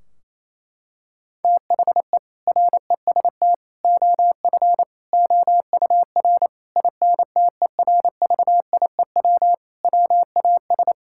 Morse code, heard as a keyed tone, about 28 wpm, 700 Hz. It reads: THE REST OF OUR INTERVIEW WAS
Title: Sentences from Top 1600 Words 28wpm